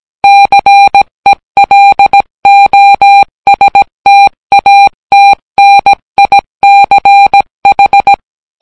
Morse code, heard as a keyed tone, about 17 wpm, 800 Hz. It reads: CELOSTATNICH